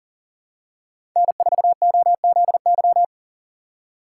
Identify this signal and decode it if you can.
Morse code, heard as a keyed tone, about 40 wpm, 700 Hz. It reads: N4OZY
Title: USA Call Signs - 5 Characters - 3x Character Spacing - 40wpm